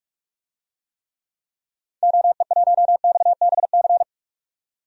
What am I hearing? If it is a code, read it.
Morse code, heard as a keyed tone, about 46 wpm, 700 Hz. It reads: OE1XBC
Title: DXCC-100 Call Signs - 6 Characters - 45wpm